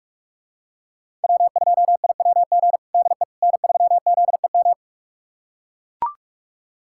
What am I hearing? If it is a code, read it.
Morse code, heard as a keyed tone, about 45 wpm, 700 Hz. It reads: W2IWG DE N3ZEM
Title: USA Callsign DE Callsign - 5 Characters - 45wpm